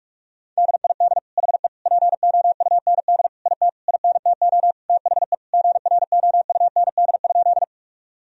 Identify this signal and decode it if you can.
Morse code, heard as a keyed tone, about 46 wpm, 700 Hz. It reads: DID HE POUND IT INTO THE GROUND?
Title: Sentences from Top 1000 Words - Review Part 2 - 45wpm